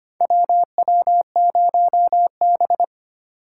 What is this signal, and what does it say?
Morse code, heard as a keyed tone, about 25 wpm, 700 Hz. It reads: WW0B
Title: USA Call Signs - 4 Characters - 2x Character Spacing - 25wpm